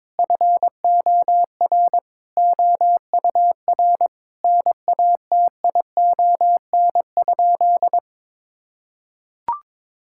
Morse code, heard as a keyed tone, about 22 wpm, 700 Hz. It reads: FOR OUR NATION?
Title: Sentences from Top 1000 Words - Review Part 6 - 22wpm